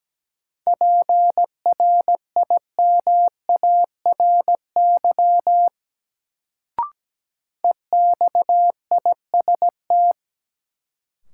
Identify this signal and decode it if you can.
Morse code, heard as a keyed tone, about 17 wpm, 700 Hz. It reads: PRIMARY EXIST
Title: Sentences from Top 1000 Words - Review Part 5 - New Word Drill - 17wpm